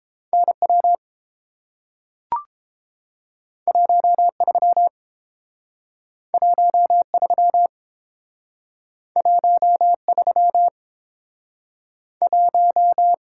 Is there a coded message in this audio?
Yes: Morse code from 0.3 to 13.1 s, heard as a keyed tone, about 30 wpm, 700 Hz. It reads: N W 1 3 1 3 1 3 1